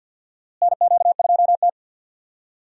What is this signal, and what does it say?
Morse code, heard as a keyed tone, about 53 wpm, 700 Hz. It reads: NQ2T
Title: USA Call Signs - 4 Characters - Repeated 3x - 50wpm